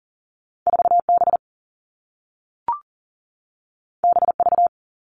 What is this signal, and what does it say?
Morse code, heard as a keyed tone, about 42 wpm, 700 Hz. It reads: 4B BV
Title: All Cut Numbers - 2-Digit 40wpm